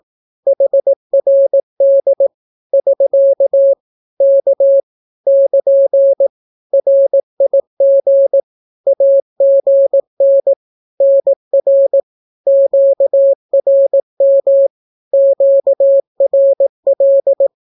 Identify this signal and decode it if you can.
Morse code, heard as a keyed tone, about 18 wpm, 550 Hz. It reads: HRD <SK> K <KN> RIG AGN NR QRM QRL
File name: naukapodstaw / skroty / skroty